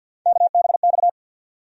Morse code, heard as a keyed tone, about 50 wpm, 700 Hz. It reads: KBX
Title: USA Call Sign Postfixes - 3 Letters 50wpm